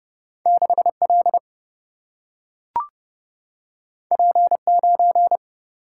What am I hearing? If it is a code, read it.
Morse code, heard as a keyed tone, about 30 wpm, 700 Hz. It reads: 6L P9
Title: Single-Letter Single-Number 30wpm